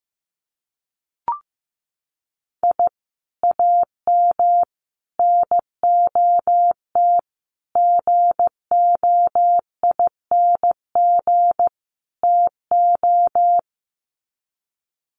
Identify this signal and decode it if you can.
Morse code, heard as a keyed tone, about 15 wpm, 700 Hz. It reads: I AM NOT GOING TO
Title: Sets of 5 Words Encore 15wpm